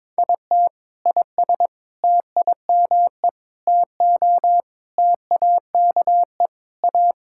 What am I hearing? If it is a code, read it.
Morse code, heard as a keyed tone, about 22 wpm, 700 Hz. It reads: IT IS TIME TO TAKE A
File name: practice / 22wpm / Sentences with 4-Word Phrases - 22wpm